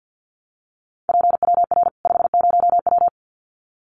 Morse code, heard as a keyed tone, about 50 wpm, 700 Hz. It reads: PWR 50W
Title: Common QSO Phrases 50wpm